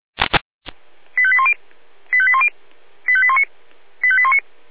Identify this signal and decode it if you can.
Five-tone selective call (ZVEI), heard as quick runs of five tones, tones of about 70 ms. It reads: 85810 85810 85819 85819